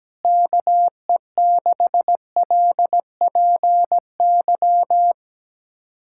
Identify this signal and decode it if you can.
Morse code, heard as a keyed tone, about 17 wpm, 700 Hz. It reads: KE6LPY
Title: USA Callsign DE Callsign - 6 Characters - 17wpm